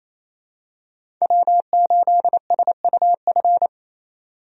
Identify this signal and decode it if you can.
Morse code, heard as a keyed tone, about 28 wpm, 700 Hz. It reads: W8SUF
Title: USA Call Signs - 5 Characters - 28wpm